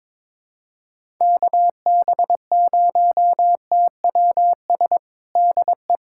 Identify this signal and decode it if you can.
Morse code, heard as a keyed tone, about 22 wpm, 700 Hz. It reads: KB0TWS DE
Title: USA Callsign DE Callsign - 6 Characters - 22wpm